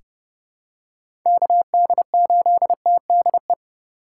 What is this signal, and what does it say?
Morse code, heard as a keyed tone, about 30 wpm, 700 Hz. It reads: KD8TDE
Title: USA Call Signs - 6 Characters - 3x Character Spacing - 30wpm